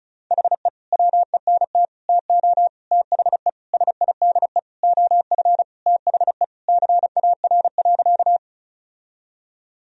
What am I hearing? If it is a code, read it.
Morse code, heard as a keyed tone, about 36 wpm, 700 Hz. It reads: HE WENT TO THE SIDE OF THE CAR.
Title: Sentences from Top 1000 Words - Review Part 1 - 35wpm